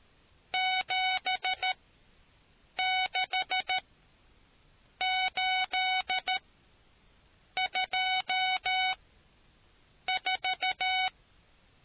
Morse code, heard as a keyed tone, about 13 wpm, spaced as for 6 wpm, 750 Hz. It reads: 76824